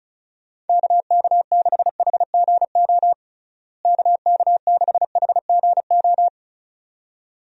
Morse code, heard as a keyed tone, about 34 wpm, 700 Hz. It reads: KK6HGO KK6HGO
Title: USA Call Signs - 6 Characters - Repeated 2x - 35wpm